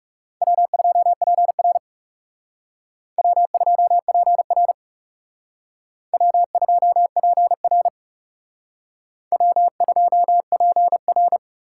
Morse code, heard as a keyed tone, about 37 wpm, 700 Hz. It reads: W2PR W2PR W2PR W2PR
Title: Speed Racing - USA 1x2 Call Signs 30wpm